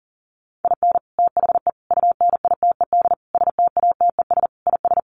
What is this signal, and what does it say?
Morse code, heard as a keyed tone, about 40 wpm, 700 Hz. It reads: IN THE UNITED STATES IS
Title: Sets of 5 Words 40wpm